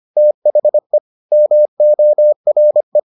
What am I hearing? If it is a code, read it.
Morse code, heard as a keyed tone, about 25 wpm, 600 Hz. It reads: THE MORE